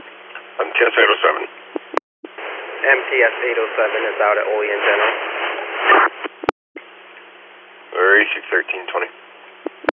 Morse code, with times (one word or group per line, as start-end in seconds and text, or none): none